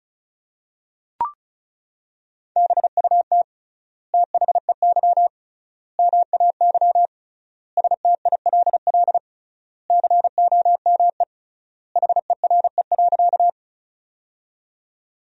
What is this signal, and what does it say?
Morse code, heard as a keyed tone, about 36 wpm, 700 Hz. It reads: BUT THEY MAY STILL COME HERE.
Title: 3x Word-Spacing - Sentences from Top 300 Words - 35wpm